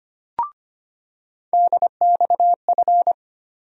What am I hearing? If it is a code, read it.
Morse code, heard as a keyed tone, about 25 wpm, 700 Hz. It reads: DXF
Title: USA Call Sign Postfixes - 3 Letters 25wpm